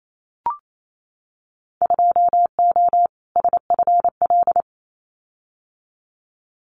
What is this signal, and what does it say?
Morse code, heard as a keyed tone, about 28 wpm, 700 Hz. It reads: 2O SFL
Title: Winter Field Day Exchanges 28wpm